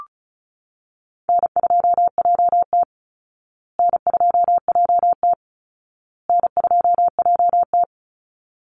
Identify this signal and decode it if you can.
Morse code, heard as a keyed tone, about 34 wpm, 700 Hz. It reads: N2JT N2JT N2JT
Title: Easy Call Signs Repeated 3x - 35wpm